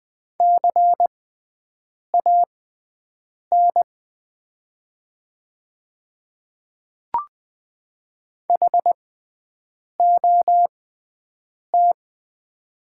Morse code, heard as a keyed tone, about 20 wpm, spaced as for 7 wpm, 700 Hz. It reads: CAN HOT
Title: 6x Character Spacing - Top 100 Words - 20wpm